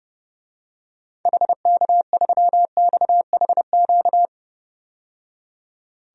Morse code, heard as a keyed tone, about 32 wpm, 700 Hz. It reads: HK3XHQ
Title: DXCC-100 Call Signs - 6 Characters - 30wpm